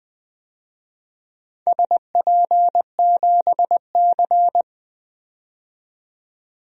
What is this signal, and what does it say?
Morse code, heard as a keyed tone, about 20 wpm, 700 Hz. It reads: SP7C
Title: DXCC-100 Call Signs - 4 Characters - 20wpm